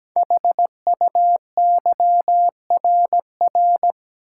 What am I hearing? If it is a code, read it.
Morse code, heard as a keyed tone, about 17 wpm, 700 Hz. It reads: HUYRR
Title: Code Groups - 5 Letters - 17wpm